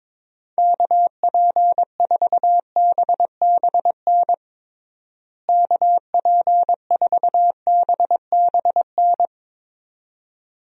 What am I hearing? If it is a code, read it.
Morse code, heard as a keyed tone, about 22 wpm, 700 Hz. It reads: KP4BBN KP4BBN
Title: DXCC-100 Call Signs - 6 Characters - Repeated 2x - 22wpm